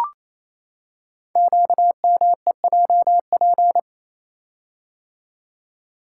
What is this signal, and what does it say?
Morse code, heard as a keyed tone, about 27 wpm, 700 Hz. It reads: QMEJP
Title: Code Groups - 5 Letter:Numbers - 28wpm